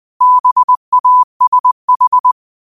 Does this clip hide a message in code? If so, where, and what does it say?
Yes: Morse code from 0.2 to 2.3 s, heard as a keyed tone, about 20 wpm, 1000 Hz. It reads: BASH